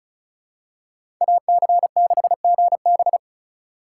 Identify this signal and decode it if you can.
Morse code, heard as a keyed tone, about 36 wpm, 700 Hz. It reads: AC6GB
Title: USA Call Signs - 5 Characters - 4x Character Spacing - 35wpm